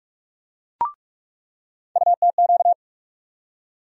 Morse code, heard as a keyed tone, about 46 wpm, 700 Hz. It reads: UTQ
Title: USA Call Sign Postfixes - 3 Letters 45wpm